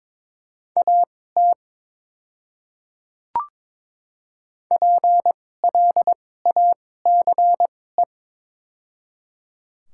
Morse code, heard as a keyed tone, about 22 wpm, spaced as for 16 wpm, 700 Hz. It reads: AT PLACE
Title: 2x Character Spacing - Top 200 Words - 22wpm